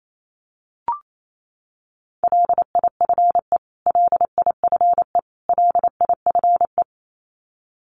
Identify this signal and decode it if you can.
Morse code, heard as a keyed tone, about 28 wpm, 700 Hz. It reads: LIFE LIFE LIFE
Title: Top 500 Words Repeated 28wpm